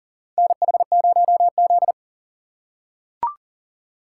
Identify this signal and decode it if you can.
Morse code, heard as a keyed tone, about 40 wpm, 700 Hz. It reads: NH0Z
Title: USA Call Signs - 4 Characters - 3x Character Spacing - 40wpm